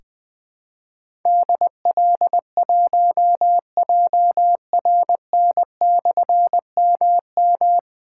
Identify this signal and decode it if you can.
Morse code, heard as a keyed tone, about 21 wpm, 700 Hz. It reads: DL1JRN/MM
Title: Difficult Call Signs 20wpm